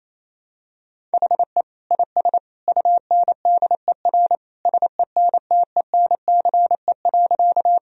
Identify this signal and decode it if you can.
Morse code, heard as a keyed tone, about 28 wpm, 700 Hz. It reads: HE IS UNDER SENTENCE.
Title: Sentences from Top 1000 Words - Review Part 1 - 28wpm